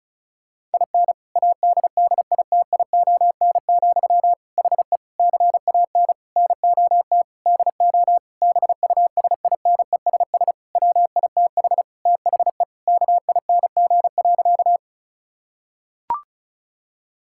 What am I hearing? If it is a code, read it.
Morse code, heard as a keyed tone, about 36 wpm, 690 Hz. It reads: IN ADDITION, HE CAN NOT DO BUSINESS WITH THE KING.
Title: Sentences from Top 1000 Words - Review Part 6 - 35wpm